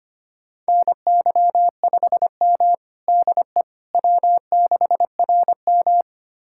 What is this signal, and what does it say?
Morse code, heard as a keyed tone, about 25 wpm, 700 Hz. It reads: NY5M DE W6RM
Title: USA Callsign DE Callsign - 4 Characters - 25wpm